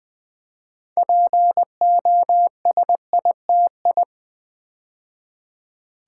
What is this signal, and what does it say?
Morse code, heard as a keyed tone, about 20 wpm, 700 Hz. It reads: POSITI
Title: Build A Word - 2 Letters At A Time - 8-Letter Words - 20wpm